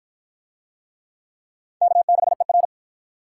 Morse code, heard as a keyed tone, about 49 wpm, 700 Hz. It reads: K6ER